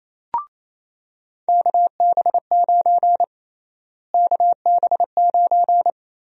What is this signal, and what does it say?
Morse code, heard as a keyed tone, about 28 wpm, 700 Hz. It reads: KB9 KB9
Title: USA Call Sign Prefixes - Repeated 2x - 28wpm